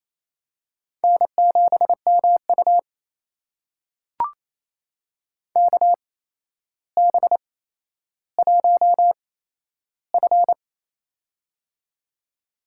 Morse code, heard as a keyed tone, about 28 wpm, spaced as for 12 wpm, 700 Hz. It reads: N7MU KB1F
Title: USA Call Signs - 8x Character Spacing - 28wpm